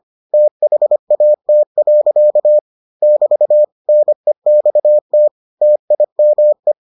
Morse code, heard as a keyed tone, about 25 wpm, 600 Hz. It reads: THAT. <BT> NEXT TIME